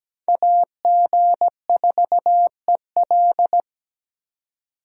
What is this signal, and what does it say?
Morse code, heard as a keyed tone, about 20 wpm, 700 Hz. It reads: AG4EL
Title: Speed Racing - USA 2x2 Call Signs 17wpm